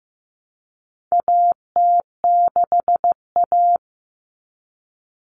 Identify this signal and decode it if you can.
Morse code, heard as a keyed tone, about 15 wpm, 700 Hz. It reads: AT6A
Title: Common Cut Numbers - 4-Digit 15wpm